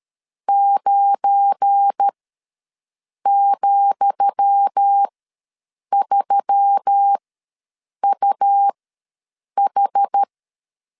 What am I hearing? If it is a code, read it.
Morse code, heard as a keyed tone, about 13 wpm, spaced as for 7 wpm, 800 Hz. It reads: 9,3UH
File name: LEZIONI / 10 SIGNS / 10D COMA ws 12wpm 3618